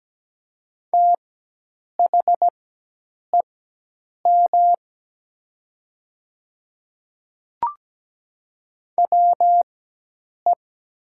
Morse code, heard as a keyed tone, about 17 wpm, spaced as for 8 wpm, 700 Hz. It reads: THEM WE
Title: 4x Character Spacing - Top 100 Words - 17wpm